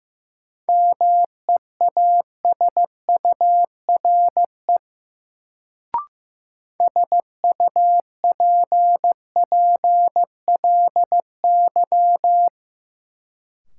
Morse code, heard as a keyed tone, about 15 wpm, 700 Hz. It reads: MEASURE SUPPLY
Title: Sentences from Top 1200 Words - New Word Drill Alt - 15wpm